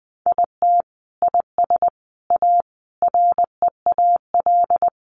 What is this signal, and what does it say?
Morse code, heard as a keyed tone, about 20 wpm, 700 Hz. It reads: IT IS A REAL
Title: Sentences from Top 1400 Words 20wpm